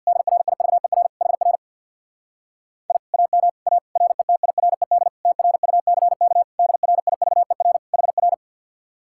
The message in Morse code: DRIVER HR I AM A RETIRED TRUCK DRIVER HR

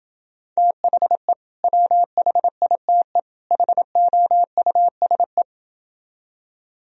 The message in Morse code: THE WHITE HOUSE

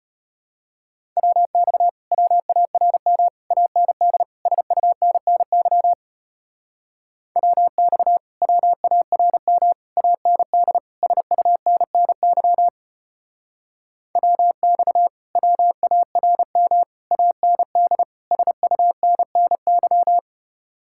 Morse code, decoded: WX WARM AND SUNNY WX WARM AND SUNNY WX WARM AND SUNNY

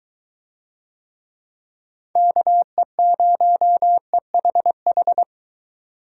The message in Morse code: KE0EHH

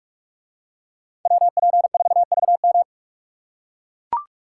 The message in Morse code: WP4VM

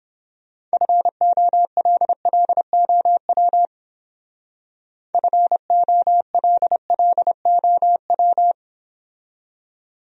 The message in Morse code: FOLLOW FOLLOW

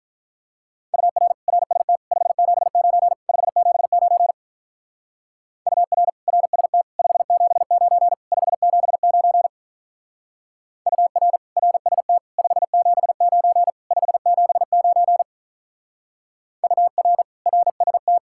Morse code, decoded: UR RST 579 579 UR RST 579 579 UR RST 579 579 UR RST

